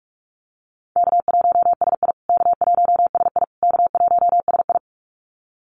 Morse code, heard as a keyed tone, about 44 wpm, 700 Hz. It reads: K1SI K1SI K1SI
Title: USA Call Signs Repeated 3x 45wpm